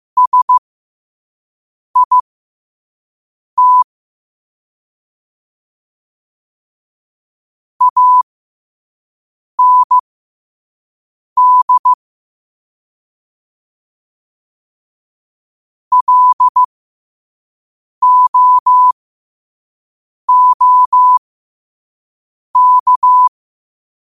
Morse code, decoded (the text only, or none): SIT AND LOOK